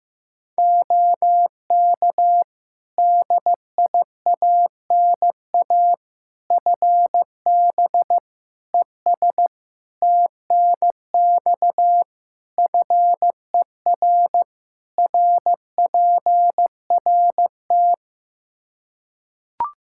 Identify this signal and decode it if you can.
Morse code, heard as a keyed tone, about 16 wpm, 700 Hz. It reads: OK DIANA FB ES TNX FER RPRT
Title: Practice QSO - Exchange 2 - Info - 15wpm